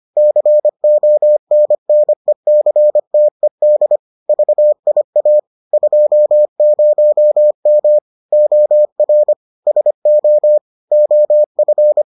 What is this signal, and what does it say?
Morse code, heard as a keyed tone, about 25 wpm, 600 Hz. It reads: CONNECTED VIA 20M OR SO OF